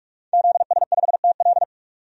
Morse code, heard as a keyed tone, about 47 wpm, 700 Hz. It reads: ZS5TL
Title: DXCC-100 Call Signs - 5 Characters - 45wpm